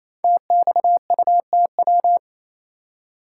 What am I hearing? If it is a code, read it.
Morse code, heard as a keyed tone, about 27 wpm, 690 Hz. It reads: TXUTW
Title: Code Groups - 5 Letters - 28wpm